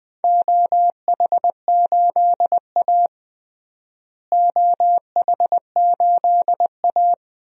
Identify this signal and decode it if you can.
Morse code, heard as a keyed tone, about 20 wpm, 700 Hz. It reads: OH8A OH8A
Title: DXCC-100 Call Signs - 4 Characters - Repeated 2x - 20wpm